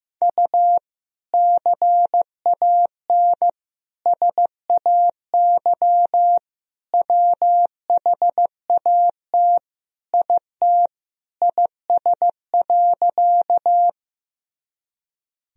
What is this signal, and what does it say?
Morse code, heard as a keyed tone, about 15 wpm, 700 Hz. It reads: U CAN SAY WHAT IT IS.
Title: Sentences from Top 200 Words and Abbreviations 15wpm